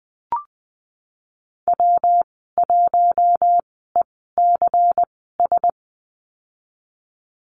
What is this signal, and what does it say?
Morse code, heard as a keyed tone, about 19 wpm, spaced as for 14 wpm, 700 Hz. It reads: W1ECS
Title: 122 Lesson 20g - Number 3 - Call Signs 20wpm